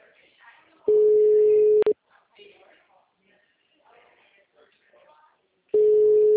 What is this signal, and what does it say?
Ringback tone (Europe, 425 Hz), with speech.